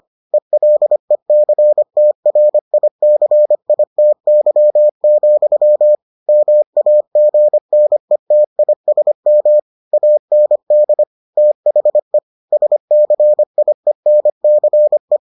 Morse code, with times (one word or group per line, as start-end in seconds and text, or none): 0.3-6.0 ELECTRICITY,
6.3-9.6 MAGNETISM
9.9-11.0 AND
11.4-12.2 THE
12.5-15.2 SCIENCE